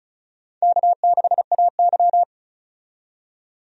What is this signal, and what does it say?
Morse code, heard as a keyed tone, about 36 wpm, 700 Hz. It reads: K6AY
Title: USA Call Signs - 4 Characters - 35wpm